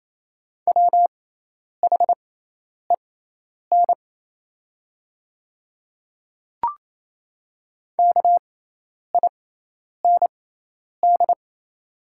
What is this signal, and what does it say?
Morse code, heard as a keyed tone, about 28 wpm, spaced as for 10 wpm, 700 Hz. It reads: WHEN KIND
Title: 6x Character Spacing - Top 200 Words - 28wpm